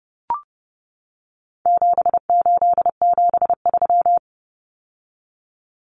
Morse code, heard as a keyed tone, about 30 wpm, 700 Hz. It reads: 7873
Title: Common Cut Numbers - 4-Digit 30wpm